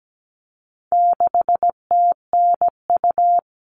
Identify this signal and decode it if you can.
Morse code, heard as a keyed tone, about 17 wpm, 700 Hz. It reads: 6TNU
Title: All Cut Numbers - 4-Digit 17wpm